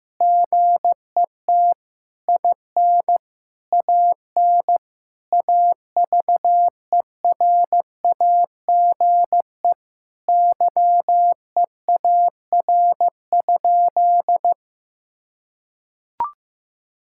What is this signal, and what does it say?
Morse code, heard as a keyed tone, about 16 wpm, 700 Hz. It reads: GET IN AN AVERAGE YEAR?
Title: Sentences from Top 1000 Words - Review Part 6 - 15wpm